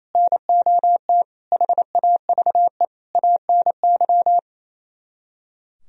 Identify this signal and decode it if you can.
Morse code, heard as a keyed tone, about 33 wpm, 700 Hz. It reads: NOT HAVE ANY
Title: Speed-Racing - Sets of 4 Words 28wpm